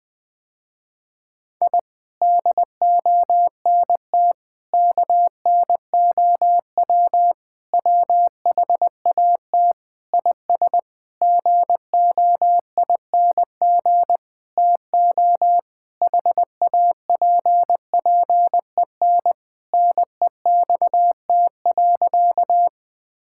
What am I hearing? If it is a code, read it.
Morse code, heard as a keyed tone, about 20 wpm, 700 Hz. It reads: I DONT KNOW WHAT IS GOING TO HAPPEN NEXT.
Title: Sentences with 7-Word Phrases - Spoken Beginning - 20wpm